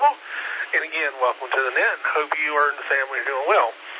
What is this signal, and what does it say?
Emergency Alert System (SAME) tones, heard as harsh buzzing bursts, with speech. It is heard from 0.2 to 3.3 s.